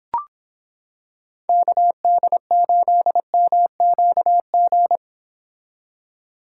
Morse code, heard as a keyed tone, about 26 wpm, 700 Hz. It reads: KD8MQG